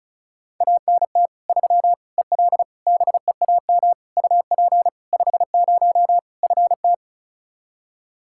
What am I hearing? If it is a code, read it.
Morse code, heard as a keyed tone, about 36 wpm, 700 Hz. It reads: ANT 3 EL BEAM UP 50 FT
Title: Practice QSO - Exchange 2 - Info - 35wpm